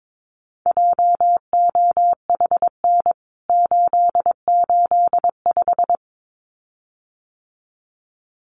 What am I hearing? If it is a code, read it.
Morse code, heard as a keyed tone, about 22 wpm, 700 Hz. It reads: JOHN 885
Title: CWT Exchange 22wpm